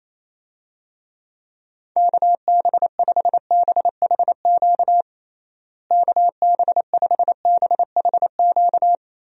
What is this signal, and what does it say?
Morse code, heard as a keyed tone, about 28 wpm, 700 Hz. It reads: KB5BHQ KB5BHQ
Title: USA Call Signs - 6 Characters - Repeated 3x - 28wpm